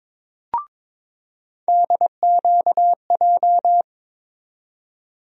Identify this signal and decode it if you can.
Morse code, heard as a keyed tone, about 22 wpm, 700 Hz. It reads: DQJ
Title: USA Call Sign Postfixes - 3 Letters 22wpm